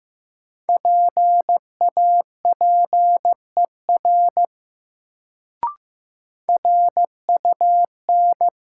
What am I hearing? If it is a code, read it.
Morse code, heard as a keyed tone, about 15 wpm, 700 Hz. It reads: PAPER RUN
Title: Sentences from Top 300 Words - New Word Drill - 15wpm